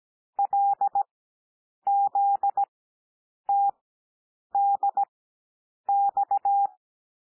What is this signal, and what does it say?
Morse code, heard as a keyed tone, about 17 wpm, spaced as for 8 wpm, 800 Hz. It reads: LZTDX
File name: cwmp3 / L2 SOUND LETTERS 17 wpm